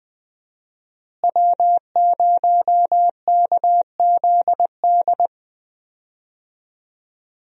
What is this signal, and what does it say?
Morse code, heard as a keyed tone, about 20 wpm, 700 Hz. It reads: W0KZD
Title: USA Call Signs - 5 Characters - Repeated 2x - 20wpm